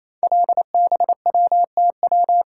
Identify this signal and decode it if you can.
Morse code, heard as a keyed tone, about 27 wpm, 690 Hz. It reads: LBWTW
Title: Code Groups - 5 Letter:Numbers - 28wpm